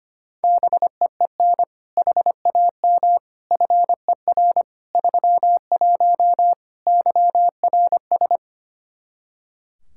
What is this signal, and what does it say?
Morse code, heard as a keyed tone, about 25 wpm, 700 Hz. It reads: BEEN HAM FER 31 YRS
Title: K1USN SST QSO Practice 25wpm